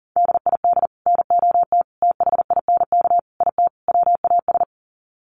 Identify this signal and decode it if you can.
Morse code, heard as a keyed tone, about 40 wpm, 700 Hz. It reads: DID NOT THINK IT WAS
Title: Sets of 5 Words Encore 40wpm